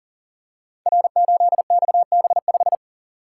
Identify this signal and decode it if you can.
Morse code, heard as a keyed tone, about 39 wpm, 700 Hz. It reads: R8XB5
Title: Code Groups - 5 Letter:Numbers - 40wpm